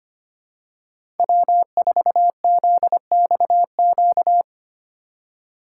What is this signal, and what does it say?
Morse code, heard as a keyed tone, about 25 wpm, 700 Hz. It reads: W4ZXQ
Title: USA Call Signs - 5 Characters - 4x Character Spacing - 25wpm